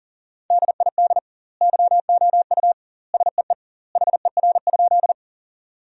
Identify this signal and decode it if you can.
Morse code, heard as a keyed tone, about 37 wpm, 700 Hz. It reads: DID YOU SEE HER?
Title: 2x Word-Spacing - Sentences from Top 100 Words - 40wpm